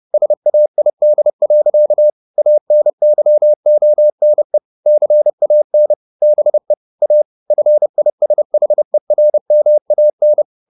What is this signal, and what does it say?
Morse code, heard as a keyed tone, about 30 wpm, 600 Hz. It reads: SAID. ANYONE CAN BE A FISHERMAN
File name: the old man and the sea / the old man and the sea 1 30wpm